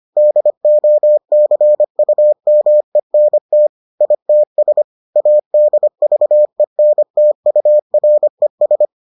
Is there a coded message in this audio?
Yes: Morse code from 0.2 to 8.9 s, heard as a keyed tone, about 25 wpm, 600 Hz. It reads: DOCUMENT ITS ADVENTURES